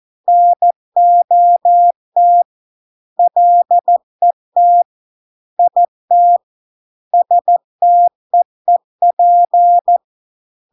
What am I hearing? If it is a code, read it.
Morse code, heard as a keyed tone, about 14 wpm, 700 Hz. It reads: NOT LET IT STEEP